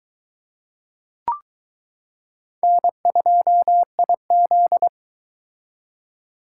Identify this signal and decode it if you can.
Morse code, heard as a keyed tone, about 22 wpm, 700 Hz. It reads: N2IZ